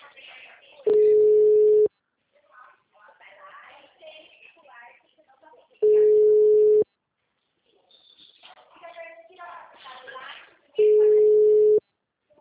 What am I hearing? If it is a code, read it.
Ringback tone (Europe, 425 Hz), with speech.